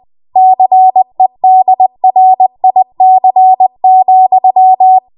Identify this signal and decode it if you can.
Morse code, heard as a keyed tone, about 20 wpm, 750 Hz. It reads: CEDRIC,
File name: Morse / 20WPM